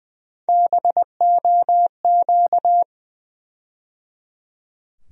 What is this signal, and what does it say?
Morse code, heard as a keyed tone, about 20 wpm, 700 Hz. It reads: BOQ
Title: USA Call Sign Postfixes - 3 Letters 20wpm